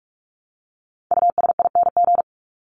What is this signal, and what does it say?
Morse code, heard as a keyed tone, about 44 wpm, 700 Hz. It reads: USING
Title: ING Suffix - Top 100 Words with Base Words 45wpm